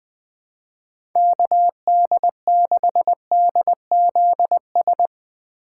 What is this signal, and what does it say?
Morse code, heard as a keyed tone, about 20 wpm, 700 Hz. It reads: KD6DZS
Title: USA Call Signs - 3x Character Spacing - 20wpm